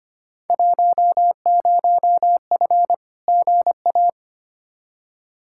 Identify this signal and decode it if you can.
Morse code, heard as a keyed tone, about 25 wpm, 700 Hz. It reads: 10F GA
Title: Field Day Exchanges Encore - 25wpm